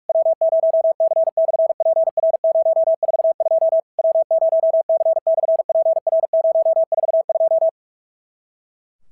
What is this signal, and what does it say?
Morse code, heard as a keyed tone, about 46 wpm, 650 Hz. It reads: W0C/PR042 W0C/PR042
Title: SOTA QSO Practice 45wpm